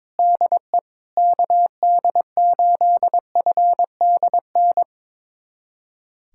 Morse code, heard as a keyed tone, about 22 wpm, 700 Hz. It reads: DE KD8FDN
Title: USA Callsign DE Callsign - 6 Characters - 22wpm